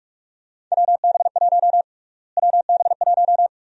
Morse code, heard as a keyed tone, about 46 wpm, 700 Hz. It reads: WB1 WB1
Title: USA Call Sign Prefixes - Repeated 2x - 45wpm